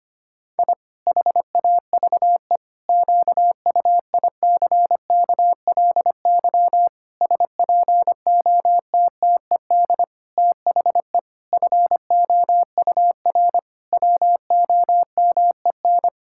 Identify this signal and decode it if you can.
Morse code, heard as a keyed tone, about 25 wpm, 700 Hz. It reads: I HAVE QUICKLY SPOTTED THE FOUR WOMEN
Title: Pangram Challenge v2 - 25wpm